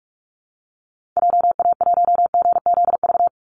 Morse code, heard as a keyed tone, about 44 wpm, 700 Hz. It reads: JA1GZV
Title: Call Signs Repeated 2x - 45wpm